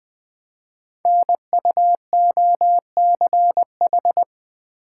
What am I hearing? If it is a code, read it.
Morse code, heard as a keyed tone, about 21 wpm, 700 Hz. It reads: NUOCH
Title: Code Groups - 5 Letters - 20wpm